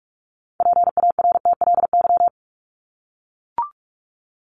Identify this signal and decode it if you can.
Morse code, heard as a keyed tone, about 44 wpm, 700 Hz. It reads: PARTLY
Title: LY Suffix - Top 100 Words with Base Words 45wpm